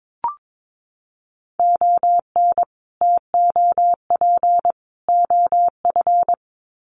Morse code, heard as a keyed tone, about 22 wpm, 700 Hz. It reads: ON TOP OF
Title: Sets of 3 Words 22wpm